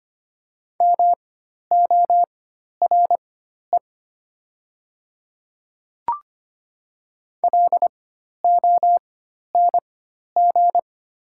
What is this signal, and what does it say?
Morse code, heard as a keyed tone, about 25 wpm, spaced as for 12 wpm, 690 Hz. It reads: MORE LONG
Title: 4x Character Spacing - Top 100 Words - 25wpm